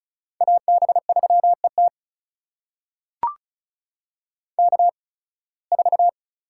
Morse code, heard as a keyed tone, about 36 wpm, spaced as for 23 wpm, 700 Hz. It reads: AB3ET K4
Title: USA Call Signs - 5 Characters - 8x Character Spacing - 35wpm